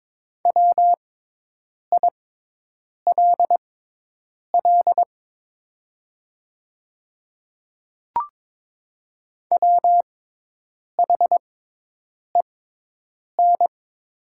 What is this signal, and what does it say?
Morse code, heard as a keyed tone, about 22 wpm, spaced as for 8 wpm, 700 Hz. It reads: WILL WHEN
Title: 6x Character Spacing - Top 100 Words - 22wpm